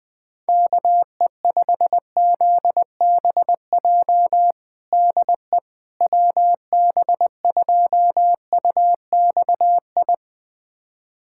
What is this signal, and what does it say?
Morse code, heard as a keyed tone, about 21 wpm, 700 Hz. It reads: KE5ZBJ DE WB2UXI
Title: USA Callsign DE Callsign 20wpm